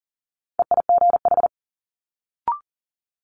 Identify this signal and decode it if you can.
Morse code, heard as a keyed tone, about 38 wpm, 700 Hz. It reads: EIGH
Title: Multi-letter Phonemes Quick Fire 40wpm